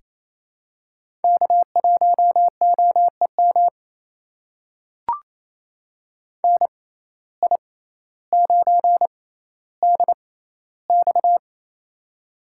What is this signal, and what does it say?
Morse code, heard as a keyed tone, about 28 wpm, spaced as for 14 wpm, 700 Hz. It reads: K1OEM NI9DX
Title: USA Call Signs - 5 Characters - 6x Character Spacing - 28wpm